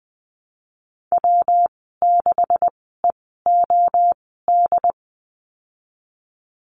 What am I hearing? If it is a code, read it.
Morse code, heard as a keyed tone, about 20 wpm, spaced as for 15 wpm, 700 Hz. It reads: W6EOD
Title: 173 Lesson 28g - Number 9 - Call Signs 20wpm